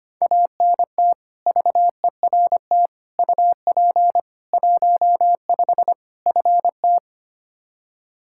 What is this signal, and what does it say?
Morse code, heard as a keyed tone, about 25 wpm, 700 Hz. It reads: ANT VERT UP 15 FT
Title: Practice QSO - Exchange 2 - Info - 25wpm